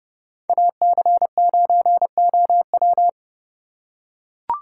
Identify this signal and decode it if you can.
Morse code, heard as a keyed tone, about 31 wpm, 700 Hz. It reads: AC9OW
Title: USA Call Signs - 5 Characters - 3x Character Spacing - 30wpm